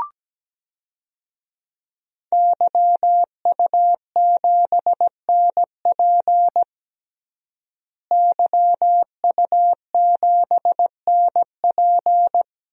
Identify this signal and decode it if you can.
Morse code, heard as a keyed tone, about 17 wpm, 700 Hz. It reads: YU7NP YU7NP
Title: DXCC-100 Call Signs - 5 Characters - Repeated 2x - 17wpm